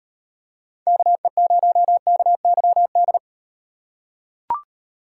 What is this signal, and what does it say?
Morse code, heard as a keyed tone, about 36 wpm, 700 Hz. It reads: KE0KYD